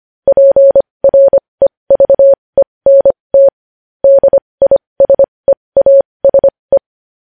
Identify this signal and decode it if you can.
Morse code, heard as a keyed tone, about 25 wpm, spaced as for 20 wpm, 550 Hz. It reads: PREVENT DISEASE
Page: news-headlines-in-morse-code-at-20-wpm